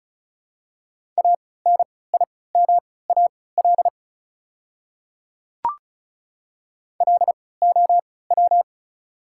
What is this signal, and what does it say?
Morse code, heard as a keyed tone, about 36 wpm, spaced as for 20 wpm, 700 Hz. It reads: ANIMAL LOW
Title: 3x Character Spacing - Top 200 Words - 35wpm